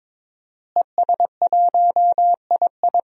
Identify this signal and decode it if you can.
Morse code, heard as a keyed tone, about 23 wpm, 700 Hz. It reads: ES1II
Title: DXCC-100 Call Signs - 5 Characters - Repeated 3x - 22wpm